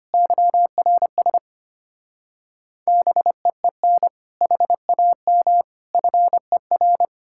Morse code, heard as a keyed tone, about 33 wpm, 700 Hz. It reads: YRS BEEN HAM FER